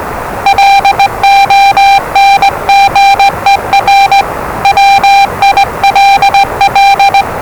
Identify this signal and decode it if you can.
Morse code, heard as a keyed tone, about 20 wpm, 790 Hz, with noise. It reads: LONGER WILL
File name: WellingtonRadioZlwCloseDownOnHfAndMf / 01-Zlw 4277khz 417.5 qrt until 0925z